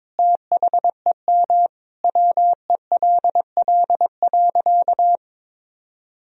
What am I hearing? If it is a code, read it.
Morse code, heard as a keyed tone, about 22 wpm, 700 Hz. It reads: THEM WELL.
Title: Sentences from Top 300 Words and Abbreviations 22wpm